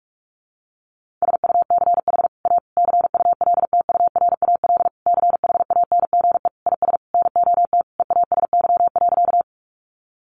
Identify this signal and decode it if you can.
Morse code, heard as a keyed tone, about 44 wpm, 700 Hz. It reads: SUCH A CULTURAL CHANGE IS NOT EASY.
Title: Sentences from Top 1500 Words 45wpm